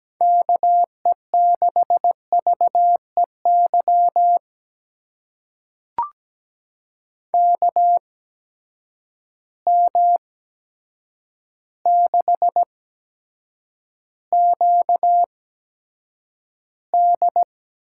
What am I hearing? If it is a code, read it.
Morse code, heard as a keyed tone, about 17 wpm, spaced as for 8 wpm, 700 Hz. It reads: KE6VEY KM6QD
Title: USA Call Signs - 6 Characters - 8x Character Spacing - 17wpm